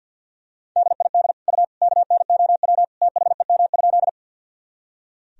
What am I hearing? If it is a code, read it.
Morse code, heard as a keyed tone, about 46 wpm, 700 Hz. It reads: DID U KNOW THEM?
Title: Sentences from Top 100 Words and Abbreviations 50wpm